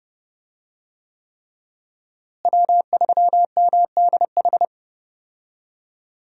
Morse code, heard as a keyed tone, about 30 wpm, 700 Hz. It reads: W3MDH
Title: USA Call Signs - 5 Characters - Repeated 3x - 30wpm